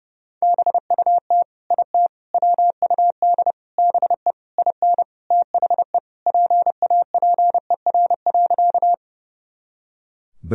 Morse code, heard as a keyed tone, about 32 wpm, 700 Hz. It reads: BUT IT WUD BE IN THE PAPER.
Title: Sentences from Top 300 Words and Abbreviations 30wpm